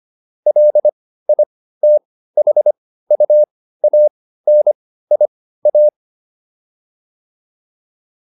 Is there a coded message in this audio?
Yes: Morse code from 0.5 to 5.9 s, heard as a keyed tone, about 25 wpm, spaced as for 15 wpm, 600 Hz. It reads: LITHUANIA